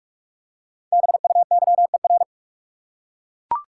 Morse code, heard as a keyed tone, about 41 wpm, 700 Hz. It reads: BUYER
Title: Five Letter Words - 45wpm